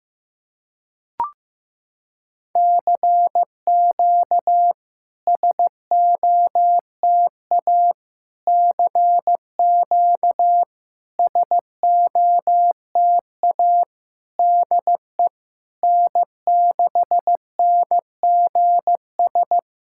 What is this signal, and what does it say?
Morse code, heard as a keyed tone, about 16 wpm, 700 Hz. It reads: CQ SOTA CQ SOTA DE N6NGS
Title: SOTA QSO Practice 15wpm